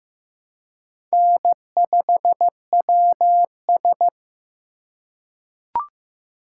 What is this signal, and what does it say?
Morse code, heard as a keyed tone, about 15 wpm, 690 Hz. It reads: N5WS
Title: USA Call Signs - 4 Characters - 2x Character Spacing - 15wpm